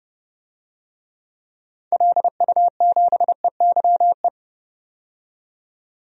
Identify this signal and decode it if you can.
Morse code, heard as a keyed tone, about 30 wpm, 700 Hz. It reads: LU7EYE
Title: DXCC-100 Call Signs - 6 Characters - 30wpm